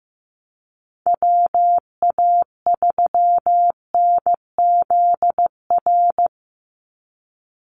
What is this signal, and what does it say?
Morse code, heard as a keyed tone, about 15 wpm, 700 Hz. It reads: WA3NZR
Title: Call Signs Repeated 3x - 15wpm